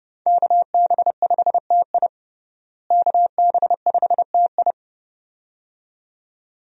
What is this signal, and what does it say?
Morse code, heard as a keyed tone, about 30 wpm, 700 Hz. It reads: KB5TI KB5TI
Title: USA Call Signs - 5 Characters - Repeated 2x - 30wpm